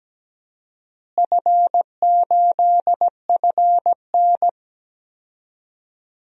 Morse code, heard as a keyed tone, about 17 wpm, 700 Hz. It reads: F8FN
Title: DXCC-100 Call Signs - 4 Characters - Repeated 2x - 17wpm